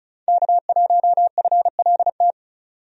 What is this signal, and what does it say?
Morse code, heard as a keyed tone, about 36 wpm, 700 Hz. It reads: K1FLT
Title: USA Call Signs - 5 Characters - 35wpm